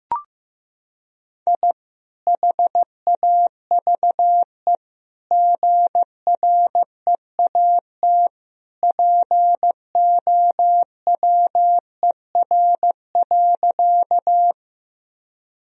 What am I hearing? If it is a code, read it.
Morse code, heard as a keyed tone, about 16 wpm, 700 Hz. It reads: I HAVE GREAT POWER.
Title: Sentences from Top 1000 Words - Review Part 2 - 15wpm